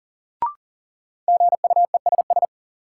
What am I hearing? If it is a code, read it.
Morse code, heard as a keyed tone, about 38 wpm, 700 Hz. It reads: GUESS
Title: Sentences from Top 1500 Words - New Word Drill - 40wpm